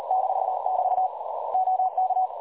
Morse code, heard as a keyed tone, about 40 wpm, 750 Hz, with noise. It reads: -<BT> OK
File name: dlazaawansowanych / nagraniazpasma / websdr recording 3565.0kHz